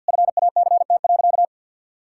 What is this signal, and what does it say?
Morse code, heard as a keyed tone, about 51 wpm, 700 Hz. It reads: FACT.
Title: Sentences from Top 1000 Words - Review Part 6 - 50wpm